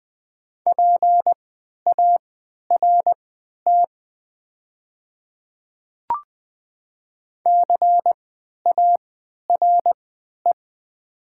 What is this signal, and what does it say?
Morse code, heard as a keyed tone, about 20 wpm, spaced as for 11 wpm, 700 Hz. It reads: PART CARE
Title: 3x Character Spacing - Top 500 Words - 20wpm